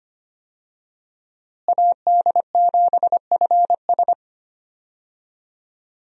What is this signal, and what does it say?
Morse code, heard as a keyed tone, about 25 wpm, 700 Hz. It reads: AD7FS
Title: USA Call Signs - 5 Characters - Repeated 2x - 25wpm